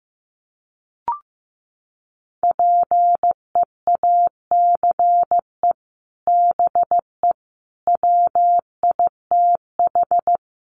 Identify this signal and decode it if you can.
Morse code, heard as a keyed tone, about 15 wpm, 700 Hz. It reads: PEACE BE WITH
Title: Sentences from Top 1500 Words 15wpm